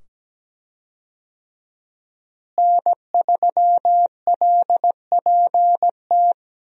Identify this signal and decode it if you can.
Morse code, heard as a keyed tone, about 17 wpm, 700 Hz. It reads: N3LPT
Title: USA Call Signs - 5 Characters - Repeated 2x - 17wpm